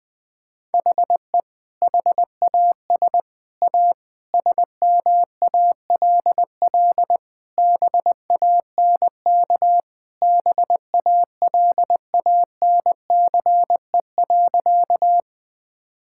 Morse code, heard as a keyed tone, about 20 wpm, 700 Hz. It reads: HE HAS A SMALL BANK BALANCE.
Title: Sentences from Top 1000 Words - Review Part 6 - 20wpm